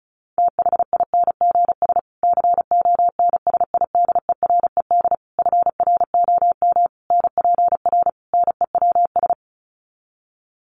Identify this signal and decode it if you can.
Morse code, heard as a keyed tone, about 34 wpm, 700 Hz. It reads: THINGS CONSIDERED FROM NPR NEWS
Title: Sets of 5 Words Encore 35wpm